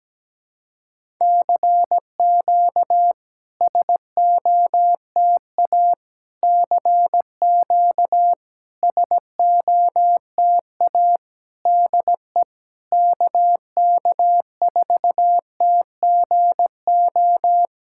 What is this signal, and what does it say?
Morse code, heard as a keyed tone, about 17 wpm, 700 Hz. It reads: CQ SOTA CQ SOTA DE KK4TGO
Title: SOTA QSO Practice 17wpm